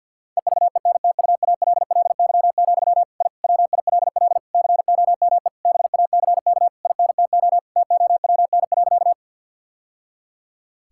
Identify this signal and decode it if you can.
Morse code, heard as a keyed tone, about 50 wpm, 700 Hz. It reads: EVENTUALLY, I WILL COME BACK INTO TOWN.
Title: Sentences from Top 1000 Words - Review Part 6 - 50wpm